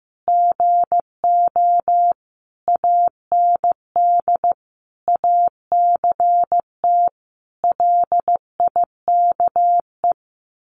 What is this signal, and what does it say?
Morse code, heard as a keyed tone, about 15 wpm, 700 Hz. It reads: GO AND ACT LIKE
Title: Sentences from Top 1200 Words 15wpm